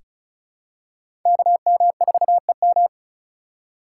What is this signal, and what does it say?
Morse code, heard as a keyed tone, about 36 wpm, 700 Hz. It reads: KM4EM
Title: USA Call Signs - 5 Characters - 6x Character Spacing - 35wpm